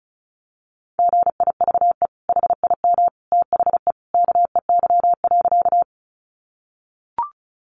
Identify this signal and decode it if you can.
Morse code, heard as a keyed tone, about 34 wpm, 700 Hz. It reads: GIVE HIM THE KEY.
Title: Sentences from Top 600 Words 35wpm